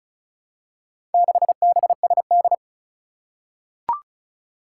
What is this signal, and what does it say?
Morse code, heard as a keyed tone, about 36 wpm, 700 Hz. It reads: 6BSD
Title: Four Challenging Letter-Number Combinations 35wpm